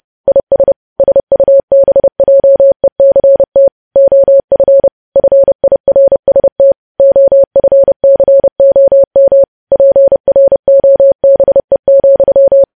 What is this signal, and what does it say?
Morse code, heard as a keyed tone, about 30 wpm, 550 Hz. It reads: IS SUBJECT OF FIRST OFCOM PROBE,
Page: news headlines in morse code at 30 wpm